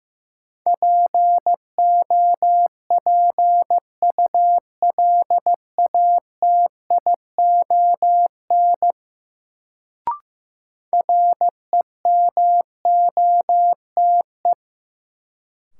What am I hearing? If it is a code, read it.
Morse code, heard as a keyed tone, about 16 wpm, 700 Hz. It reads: POPULATION REMOTE
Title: Sentences from Top 1300 Words - New Word Drill Alt - 15wpm